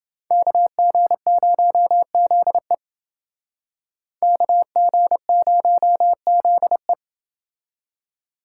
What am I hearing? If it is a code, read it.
Morse code, heard as a keyed tone, about 28 wpm, 700 Hz. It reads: KG0ZE KG0ZE